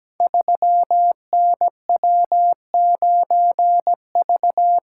Morse code, heard as a keyed tone, about 17 wpm, 700 Hz. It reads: 3NW9V
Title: Code Groups - 5 Letter:Numbers - 17wpm